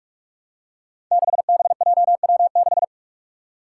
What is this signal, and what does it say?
Morse code, heard as a keyed tone, about 43 wpm, 700 Hz. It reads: 6BJW6
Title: Five Challenging Letter-Number Combinations 45wpm